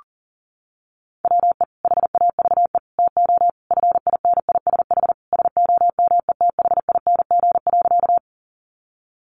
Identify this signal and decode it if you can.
Morse code, heard as a keyed tone, about 40 wpm, 700 Hz. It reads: WE HAVE TO FINISH SOMETHING.
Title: Sentences from Top 1200 Words 40wpm